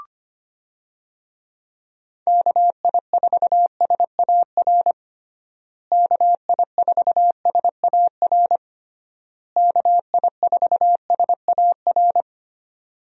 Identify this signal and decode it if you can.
Morse code, heard as a keyed tone, about 25 wpm, 700 Hz. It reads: KI4SAR KI4SAR KI4SAR
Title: USA Call Signs - 6 Characters - Repeated 3x - 25wpm